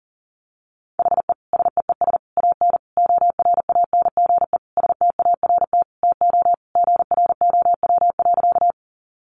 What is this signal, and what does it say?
Morse code, heard as a keyed tone, about 40 wpm, 700 Hz. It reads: HE SEES AN ORANGE START TO GROW.
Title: Easy Phrases 40wpm